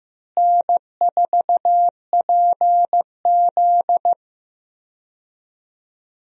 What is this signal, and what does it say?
Morse code, heard as a keyed tone, about 15 wpm, 700 Hz. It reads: N4PZ
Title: USA Call Signs - 4 Characters - Repeated 2x - 15wpm